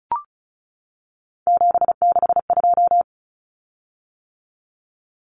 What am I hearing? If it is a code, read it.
Morse code, heard as a keyed tone, about 34 wpm, 700 Hz. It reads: 762
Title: Common Cut Numbers - 3-Digit 35wpm